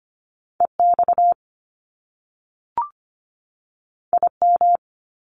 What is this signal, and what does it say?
Morse code, heard as a keyed tone, about 25 wpm, 700 Hz. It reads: EX IM
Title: Top Two Letter Combinations in Words 25wpm